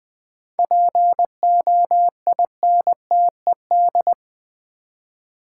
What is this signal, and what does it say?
Morse code, heard as a keyed tone, about 20 wpm, 690 Hz. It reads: POINTED
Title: 7 Letter Words - First 3 Letters Given - 20wpm